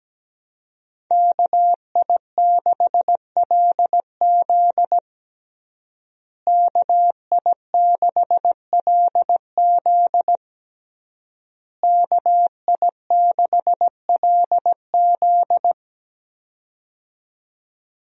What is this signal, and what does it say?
Morse code, heard as a keyed tone, about 17 wpm, 700 Hz. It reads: KI6LZ KI6LZ KI6LZ
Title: USA Call Signs - 5 Characters - Repeated 3x - 17wpm